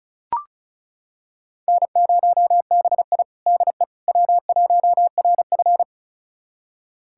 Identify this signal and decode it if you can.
Morse code, heard as a keyed tone, about 36 wpm, 700 Hz. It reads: N0BI DE W1RF
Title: USA Callsign DE Callsign - 4 Characters - 35wpm